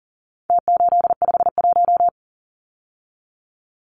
Morse code, heard as a keyed tone, about 40 wpm, 700 Hz. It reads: T851
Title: Numbers - Contest Style - 4-Digit 40wpm